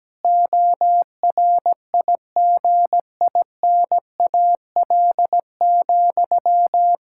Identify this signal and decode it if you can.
Morse code, heard as a keyed tone, about 17 wpm, 700 Hz. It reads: ORIGINAL,
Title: Sentences from Top 1000 Words - Review Part 6 - 17wpm